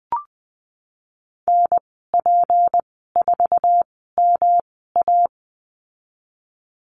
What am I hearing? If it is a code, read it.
Morse code, heard as a keyed tone, about 19 wpm, spaced as for 14 wpm, 700 Hz. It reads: NP4MA
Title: 154 Lesson 25g - Letter p - Call Signs 20wpm